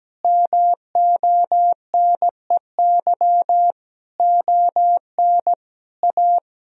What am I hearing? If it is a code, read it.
Morse code, heard as a keyed tone, about 17 wpm, 700 Hz. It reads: MONEY ON A
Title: Sentences from Top 1000 Words - Review Part 4 - 17wpm